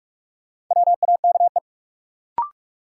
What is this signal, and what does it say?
Morse code, heard as a keyed tone, about 42 wpm, 700 Hz. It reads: WAKE
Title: Sets of 4 Words - New Word Drill - 45wpm